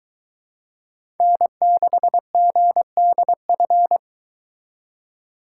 Morse code, heard as a keyed tone, about 23 wpm, 700 Hz. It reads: N6GDF